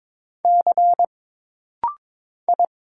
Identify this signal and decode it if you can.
Morse code, heard as a keyed tone, about 22 wpm, 700 Hz. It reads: C I
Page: This rapid-fire timing is geared to push your ICR (Instant Character Recognition) skills.